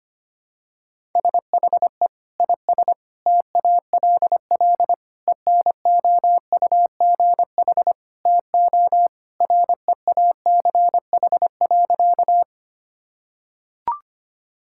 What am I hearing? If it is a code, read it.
Morse code, heard as a keyed tone, about 25 wpm, 700 Hz. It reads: SHE IS TALL ENOUGH TO REACH.
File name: practice / 25wpm / Sentences with 2-Word Phrases - 25wpm